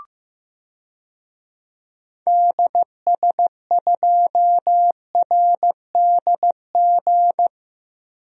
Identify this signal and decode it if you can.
Morse code, heard as a keyed tone, about 15 wpm, 700 Hz. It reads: DS2RDG
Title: DXCC-100 Call Signs - 6 Characters - 15wpm